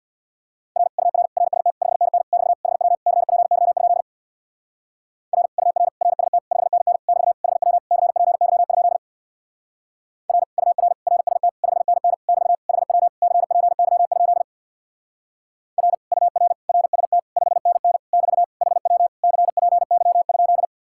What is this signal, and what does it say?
Morse code, heard as a keyed tone, about 64 wpm, 700 Hz. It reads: R UR RST 5NN <BT> HW CPY? R UR RST 5NN <BT> HW CPY? R UR RST 5NN <BT> HW CPY? R UR RST 5NN <BT> HW CPY?